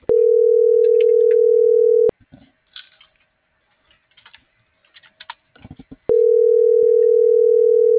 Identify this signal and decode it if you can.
Ringback tone (North America), 440 + 480 Hz.